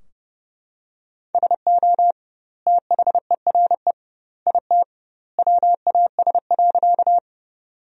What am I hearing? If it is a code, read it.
Morse code, heard as a keyed tone, about 30 wpm, 700 Hz. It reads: SO THERE IT WAS.
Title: 2x Word-Spacing - Sentences from Top 100 Words - 30wpm